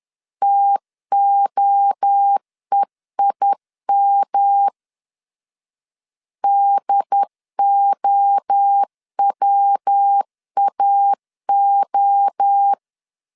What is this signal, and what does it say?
Morse code, heard as a keyed tone, about 10 wpm, 800 Hz. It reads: TOEIM DOWAO
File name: CW / DET / 1 3 1500C S B 10wpm 3335